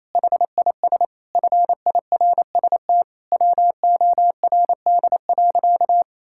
Morse code, heard as a keyed tone, about 28 wpm, 700 Hz. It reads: HIS FIRST WORD.
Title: Pangram Challenge v2 - 28wpm